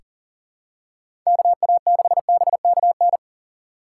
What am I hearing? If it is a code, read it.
Morse code, heard as a keyed tone, about 39 wpm, 700 Hz. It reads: KA6BKN
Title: USA Call Signs - 6 Characters - 3x Character Spacing - 40wpm